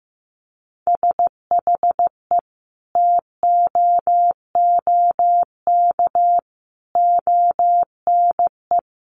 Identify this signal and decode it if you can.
Morse code, heard as a keyed tone, about 14 wpm, 700 Hz. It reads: SHE TOOK ONE
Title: Sentences from Top 600 Words 15wpm